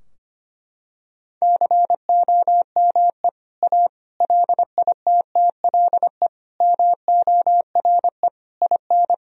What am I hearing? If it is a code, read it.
Morse code, heard as a keyed tone, about 25 wpm, 700 Hz. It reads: COME A LITTLE MORE IN
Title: Sentences from Top 1000 Words - Review Part 2 - 25wpm